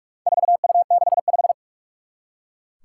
Morse code, heard as a keyed tone, about 43 wpm, 700 Hz. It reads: 4U65
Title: Four Challenging Letter-Number Combinations 45wpm